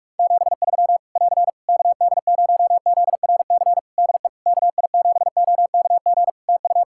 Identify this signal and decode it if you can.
Morse code, heard as a keyed tone, about 46 wpm, 700 Hz. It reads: 73 <AR> KD0ZRC DE KI7OKG TU
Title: K1USN SST QSO Practice 45wpm